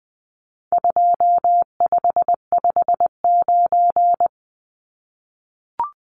Morse code, heard as a keyed tone, about 20 wpm, 700 Hz. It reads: 2559
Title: Numbers - 4-Digit 20wpm